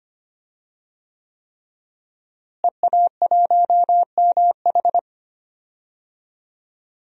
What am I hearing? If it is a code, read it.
Morse code, heard as a keyed tone, about 25 wpm, 700 Hz. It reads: EA1MH
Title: DXCC-100 Call Signs - 5 Characters - Repeated 3x - 25wpm